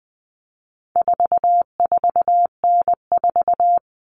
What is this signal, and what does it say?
Morse code, heard as a keyed tone, about 20 wpm, 700 Hz. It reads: 44N4
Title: All Cut Numbers - 4-Digit 20wpm